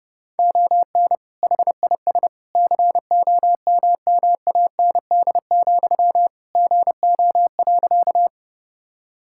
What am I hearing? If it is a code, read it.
Morse code, heard as a keyed tone, about 32 wpm, 700 Hz. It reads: ON HIS COMMAND, GO.
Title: Sentences from Top 1600 Words 30wpm